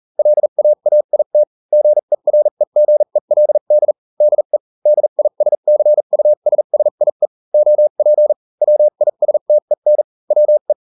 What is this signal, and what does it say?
Morse code, heard as a keyed tone, about 40 wpm, spaced as for 35 wpm, 600 Hz. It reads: LAAIT GEREGELD DE DISCUSSIE OP WISTEN WE